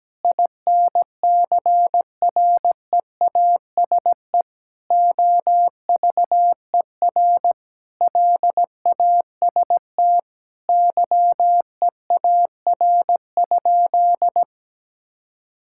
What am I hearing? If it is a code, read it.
Morse code, heard as a keyed tone, about 17 wpm, 700 Hz. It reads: INCREASE OVER LAST YEAR?
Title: Sentences from Top 1000 Words - Review Part 5 - 17wpm